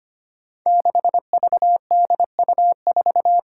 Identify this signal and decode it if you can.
Morse code, heard as a keyed tone, about 25 wpm, 700 Hz. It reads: 6VDU4
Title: Five Challenging Letter-Number Combinations 25wpm